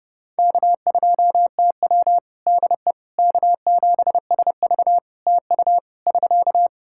Morse code, heard as a keyed tone, about 30 wpm, 700 Hz. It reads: K2TW DE K7SV TU <SK>
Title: K1USN SST QSO Practice 30wpm